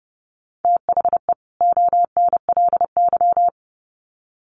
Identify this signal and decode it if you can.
Morse code, heard as a keyed tone, about 30 wpm, 700 Hz. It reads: THE ONLY
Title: Sets of 2 Words 30wpm